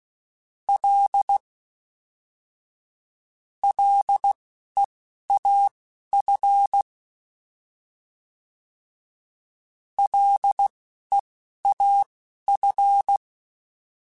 Morse code, heard as a keyed tone, about 16 wpm, spaced as for 11 wpm, 800 Hz. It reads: L LEAF LEAF